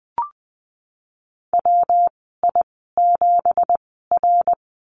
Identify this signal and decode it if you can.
Morse code, heard as a keyed tone, about 20 wpm, spaced as for 14 wpm, 700 Hz. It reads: WI7R
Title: 231 Lesson 37g - Letter x - Call Signs 20wpm